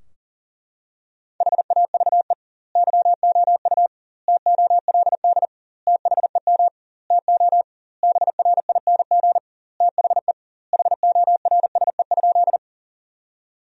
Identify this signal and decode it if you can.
Morse code, heard as a keyed tone, about 37 wpm, 700 Hz. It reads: HAVE YOU TOLD THEM TO BRING THE HORSE?
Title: 2x Word-Spacing - Sentences from Top 500 Words - 40wpm